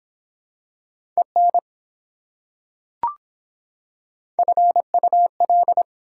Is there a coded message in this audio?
Yes: Morse code from 1.2 to 5.8 s, heard as a keyed tone, about 24 wpm, 700 Hz. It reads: EN FUL